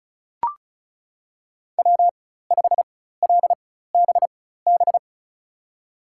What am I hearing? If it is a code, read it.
Morse code, heard as a keyed tone, about 36 wpm, spaced as for 16 wpm, 700 Hz. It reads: W5LBB
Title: USA Call Signs - 4x Character Spacing - 35wpm